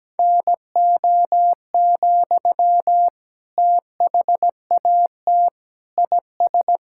Morse code, heard as a keyed tone, about 17 wpm, 700 Hz. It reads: NO, THAT IS
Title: Sentences from Top 1000 Words - Review Part 5 - 17wpm